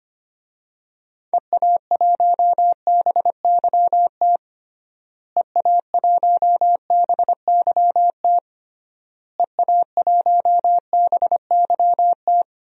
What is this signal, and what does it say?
Morse code, heard as a keyed tone, about 25 wpm, 700 Hz. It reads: EA1BYT EA1BYT EA1BYT
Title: DXCC-100 Call Signs - 6 Characters - Repeated 3x - 25wpm